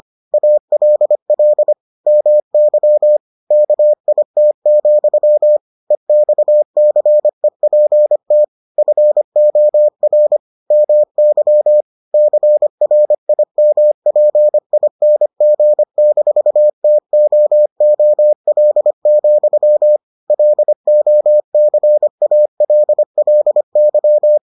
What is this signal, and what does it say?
Morse code, heard as a keyed tone, about 25 wpm, 600 Hz. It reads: ALL MY KIT, EXCEPT FOR MY CRIMPING-TOOL, LOCALLY